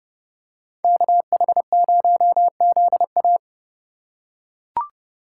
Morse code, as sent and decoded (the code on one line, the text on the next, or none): -.- .... ----- --.. .-
KH0ZA